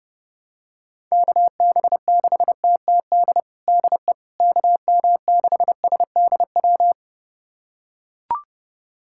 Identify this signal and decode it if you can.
Morse code, heard as a keyed tone, about 28 wpm, 700 Hz. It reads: KB6TTD DE KM6SDW
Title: USA Callsign DE Callsign - 6 Characters - 30wpm